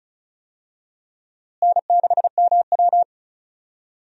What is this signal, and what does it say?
Morse code, heard as a keyed tone, about 36 wpm, 700 Hz. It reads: N6MW
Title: USA Call Signs - 4 Characters - Repeated 3x - 35wpm